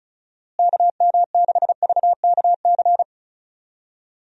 Morse code, heard as a keyed tone, about 36 wpm, 700 Hz. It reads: KM6VKC
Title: USA Call Signs - 6 Characters - Repeated 2x - 35wpm